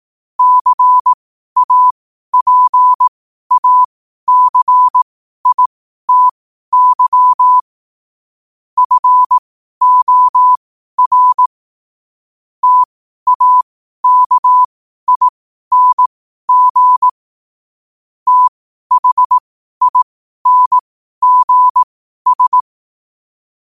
Morse code, heard as a keyed tone, about 18 wpm, spaced as for 12 wpm, 1000 Hz. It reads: CAPACITY FOR TAKING THINGS
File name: qotdfiles / 12wpm